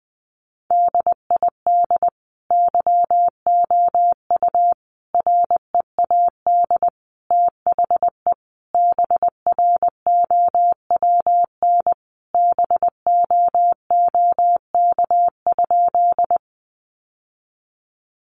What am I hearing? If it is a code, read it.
Morse code, heard as a keyed tone, about 20 wpm, 700 Hz. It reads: DID YOU READ THE BROWN BOOK?
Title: Sentences from Top 1200 Words 20wpm